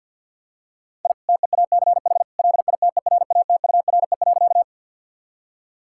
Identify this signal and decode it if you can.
Morse code, heard as a keyed tone, about 50 wpm, 700 Hz. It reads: I TEACH LITERATURE.
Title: Sentences from Top 1000 Words - Review Part 6 - 50wpm